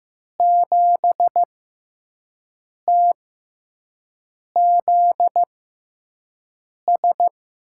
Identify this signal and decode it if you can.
Morse code, heard as a keyed tone, about 15 wpm, spaced as for 5 wpm, 700 Hz. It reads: 7TZS
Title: USA Call Signs - 5 Characters - 6x Character Spacing - 15wpm